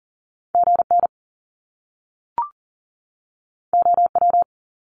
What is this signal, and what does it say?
Morse code, heard as a keyed tone, about 40 wpm, 700 Hz. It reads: GN OW
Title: Top Two Letter Combinations in Words 40wpm